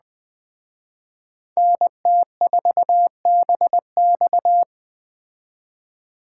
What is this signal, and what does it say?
Morse code, heard as a keyed tone, about 20 wpm, 700 Hz. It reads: NT4BX
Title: Speed Racing - USA 2x2 Call Signs 17wpm